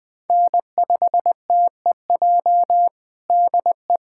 Morse code, heard as a keyed tone, about 21 wpm, 700 Hz. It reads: N5TEJ DE
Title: USA Callsign DE Callsign 20wpm